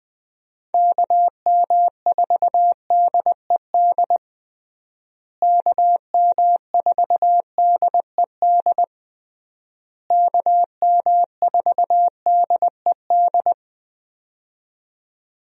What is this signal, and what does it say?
Morse code, heard as a keyed tone, about 21 wpm, 700 Hz. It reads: KM4DED KM4DED KM4DED
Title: USA Call Signs - 6 Characters - Repeated 3x - 20wpm